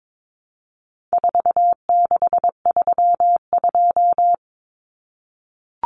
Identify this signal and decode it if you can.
Morse code, heard as a keyed tone, about 22 wpm, 700 Hz. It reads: 4632
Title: Common Cut Numbers - 4-Digit 22wpm